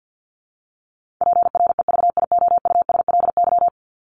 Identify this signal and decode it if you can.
Morse code, heard as a keyed tone, about 50 wpm, 700 Hz. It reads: PREVIOUSLY
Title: LY Suffix - Top 100 Words with Base Words 50wpm